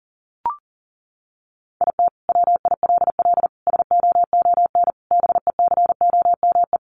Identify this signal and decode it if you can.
Morse code, heard as a keyed tone, about 40 wpm, 700 Hz. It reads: IT WILL SOON BECOME
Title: Sentences from Top 1000 Words 40wpm